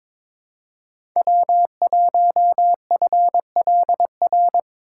Morse code, heard as a keyed tone, about 21 wpm, 700 Hz. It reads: W1FLR
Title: USA Call Signs - 5 Characters - 2x Character Spacing - 22wpm